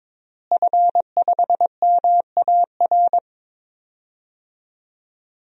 Morse code, heard as a keyed tone, about 22 wpm, 700 Hz. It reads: F5MAR
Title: DXCC-100 Call Signs - 5 Characters - Repeated 2x - 22wpm